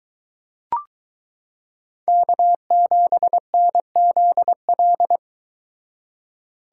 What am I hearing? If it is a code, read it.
Morse code, heard as a keyed tone, about 22 wpm, 700 Hz. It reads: K7NZL